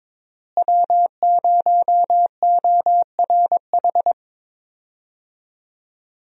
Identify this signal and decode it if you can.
Morse code, heard as a keyed tone, about 22 wpm, 700 Hz. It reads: W0ORH
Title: USA Call Signs - 5 Characters - Repeated 3x - 22wpm